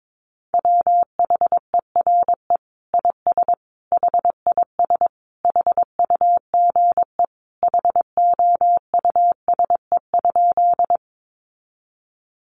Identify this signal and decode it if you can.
Morse code, heard as a keyed tone, about 22 wpm, 700 Hz. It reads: WHERE IS HIS HUGE HOUSE?
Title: Sentences from Top 1100 Words 22wpm